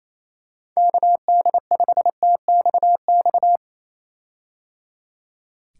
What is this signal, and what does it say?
Morse code, heard as a keyed tone, about 27 wpm, 700 Hz. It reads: KD5TXX
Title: K1USN SST QSO Practice - 28wpm